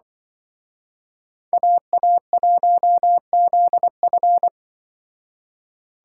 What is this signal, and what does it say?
Morse code, heard as a keyed tone, about 24 wpm, 700 Hz. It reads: AA1ZF
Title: Speed Racing - USA 2x2 Call Signs 20wpm